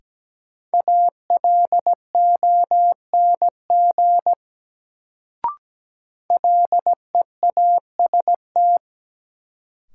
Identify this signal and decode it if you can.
Morse code, heard as a keyed tone, about 17 wpm, 700 Hz. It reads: ALONG LEAST
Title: Sentences from Top 700 Words - New Word Drill - 17wpm